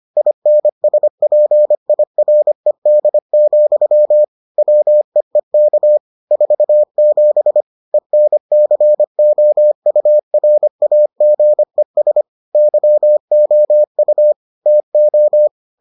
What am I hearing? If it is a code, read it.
Morse code, heard as a keyed tone, about 25 wpm, 600 Hz. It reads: INSPIRED, WEEK 47 ENCOURAGES YOU TO